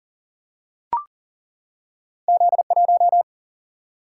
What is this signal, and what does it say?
Morse code, heard as a keyed tone, about 38 wpm, 700 Hz. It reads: Z 1
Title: Single-Letter Single-Number 40wpm